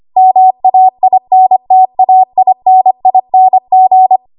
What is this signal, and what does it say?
Morse code, heard as a keyed tone, about 25 wpm, 750 Hz. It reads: MAINTAINING
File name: Morse / 25WPM